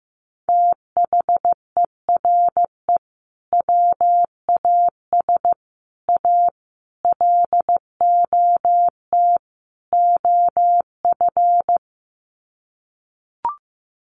Morse code, heard as a keyed tone, about 15 wpm, 700 Hz. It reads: THERE WAS A LOT OF
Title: Sets of 5 Words 15wpm